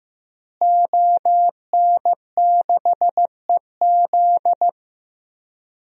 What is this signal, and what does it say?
Morse code, heard as a keyed tone, about 16 wpm, 700 Hz. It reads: ON6EZ
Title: DXCC-100 Call Signs - 5 Characters - Repeated 3x - 15wpm